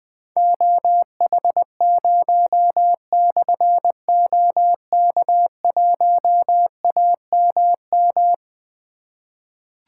Morse code, heard as a keyed tone, about 20 wpm, 700 Hz. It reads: OH0/OK1AMM
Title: Difficult Call Signs 20wpm